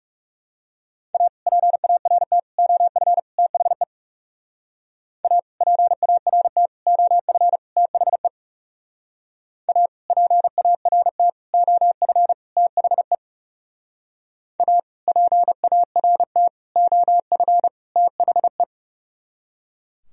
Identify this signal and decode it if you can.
Morse code, heard as a keyed tone, about 36 wpm, 700 Hz. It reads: A PART OF THE A PART OF THE A PART OF THE A PART OF THE
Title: Speed-Racing - Sets of 4 Words 30wpm